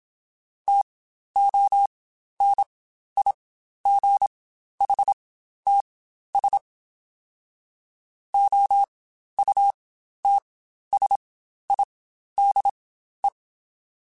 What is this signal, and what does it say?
Morse code, heard as a keyed tone, about 27 wpm, spaced as for 12 wpm, 800 Hz. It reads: TONIGHTS OUTSIDE
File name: Spaced Characters / Spaced Characters-mp3s / Weather 4 10 sp @25